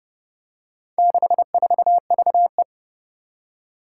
Morse code, heard as a keyed tone, about 30 wpm, 700 Hz. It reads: 64VE
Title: Four Challenging Letter-Number Combinations 30wpm